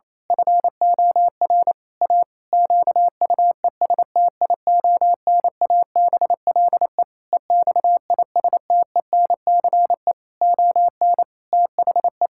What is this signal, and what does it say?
Morse code, heard as a keyed tone, about 27 wpm, 700 Hz. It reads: FOR A QUESTIONABLE EXISTENCE ON THE
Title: Pangram Challenge v2 - 28wpm